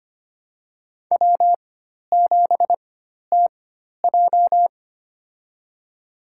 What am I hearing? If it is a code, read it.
Morse code, heard as a keyed tone, about 25 wpm, spaced as for 12 wpm, 700 Hz. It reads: W7TJ
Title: USA Call Signs - 4 Characters - 4x Character Spacing - 25wpm